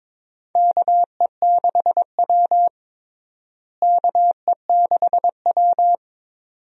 Morse code, heard as a keyed tone, about 22 wpm, 700 Hz. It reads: KE6W KE6W
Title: USA Call Signs - 4 Characters - Repeated 2x - 22wpm